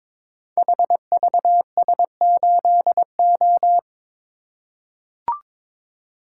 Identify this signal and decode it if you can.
Morse code, heard as a keyed tone, about 22 wpm, 700 Hz. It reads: HVS8O
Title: Code Groups - 5 Letter:Numbers - 22wpm